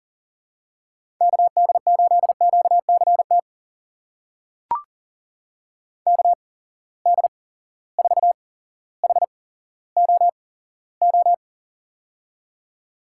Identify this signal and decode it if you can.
Morse code, heard as a keyed tone, about 38 wpm, spaced as for 17 wpm, 700 Hz. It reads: KD8QCT KD4HOO
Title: USA Call Signs - 6 Characters - 8x Character Spacing - 40wpm